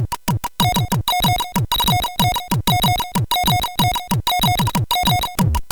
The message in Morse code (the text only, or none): none